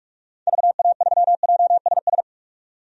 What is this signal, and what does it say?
Morse code, heard as a keyed tone, about 42 wpm, 700 Hz. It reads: VA3JSS
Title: DXCC-100 Call Signs - 6 Characters - 45wpm